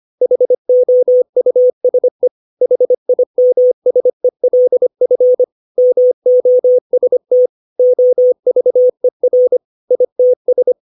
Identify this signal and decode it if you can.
Morse code, heard as a keyed tone, about 25 wpm, 500 Hz. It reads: HOUSE HIMSELF MOST OVER ITS
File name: 632 words / 632 Words 25WPM